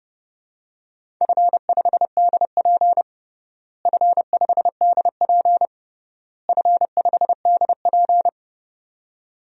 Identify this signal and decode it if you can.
Morse code, heard as a keyed tone, about 30 wpm, 700 Hz. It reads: F5DP F5DP F5DP
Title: DXCC-100 Call Signs - 4 Characters - Repeated 3x - 30wpm